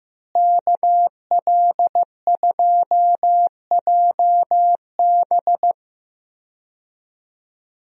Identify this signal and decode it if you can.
Morse code, heard as a keyed tone, about 15 wpm, 700 Hz. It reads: KL2JB
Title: USA Call Signs - 5 Characters - Repeated 3x - 15wpm